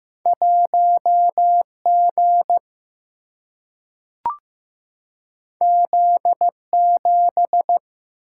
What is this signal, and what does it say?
Morse code, heard as a keyed tone, about 15 wpm, 700 Hz. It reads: 1 G Z 7
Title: Single-Letter Single-Number 15wpm